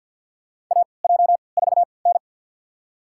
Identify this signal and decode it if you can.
Morse code, heard as a keyed tone, about 52 wpm, spaced as for 29 wpm, 700 Hz. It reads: AJ4N
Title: USA Call Signs - 3x Character Spacing - 50wpm